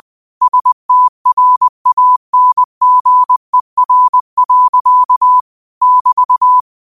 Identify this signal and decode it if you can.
Morse code, heard as a keyed tone, about 20 wpm, 1000 Hz. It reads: STRANGER. <BT>
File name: qotdfiles / 20wpm